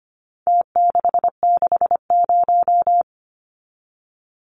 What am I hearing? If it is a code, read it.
Morse code, heard as a keyed tone, about 25 wpm, 700 Hz. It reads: T660
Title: Numbers - Contest Style - 4-Digit 25wpm